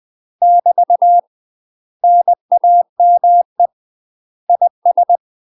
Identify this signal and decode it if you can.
Morse code, heard as a keyed tone, about 20 wpm, 700 Hz. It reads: <BT> NAME IS
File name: UK MASTER QSOs TRAINING / 20WPM / UK MASTER QSOs 0046